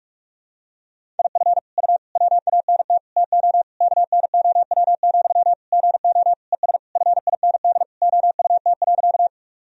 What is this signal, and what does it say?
Morse code, heard as a keyed tone, about 44 wpm, 700 Hz. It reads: IF U WANT TO KNOW, GO ES FIND OUT.
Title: Sentences from Top 200 Words and Abbreviations 45wpm